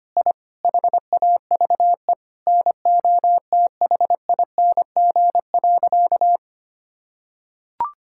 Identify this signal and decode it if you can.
Morse code, heard as a keyed tone, about 25 wpm, 700 Hz. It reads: I HAVE NOTHING.
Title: Sentences from Top 1000 Words - Review Part 6 - 25wpm